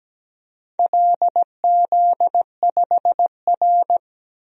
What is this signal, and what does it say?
Morse code, heard as a keyed tone, about 17 wpm, 700 Hz. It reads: LZ5R
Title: DXCC-100 Call Signs - 4 Characters - 17wpm